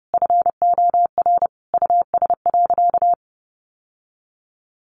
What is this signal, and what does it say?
Morse code, heard as a keyed tone, about 30 wpm, 700 Hz. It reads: FOR US.
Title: Sentences from Top 900 Words 30wpm